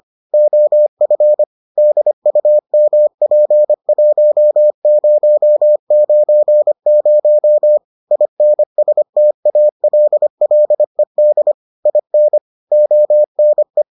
Morse code, heard as a keyed tone, about 25 wpm, 600 Hz. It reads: OF DUMP1090 INSTALLED IN ONE